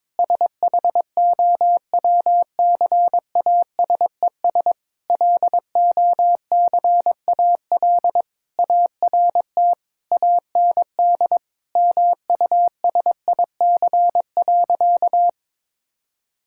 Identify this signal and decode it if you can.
Morse code, heard as a keyed tone, about 22 wpm, 700 Hz. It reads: SHOWCASES LOCAL ART AND MUSIC.
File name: Difficult Sentences with Hints - 22wpm